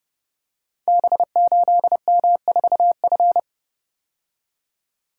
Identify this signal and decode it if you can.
Morse code, heard as a keyed tone, about 30 wpm, 700 Hz. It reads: B8M4F
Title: Code Groups - 5 Letter:Numbers - 30wpm